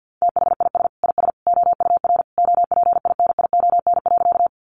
Morse code, heard as a keyed tone, about 50 wpm, 700 Hz. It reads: THIS IS OUR OPINION.
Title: Sentences from Top 1200 Words 50wpm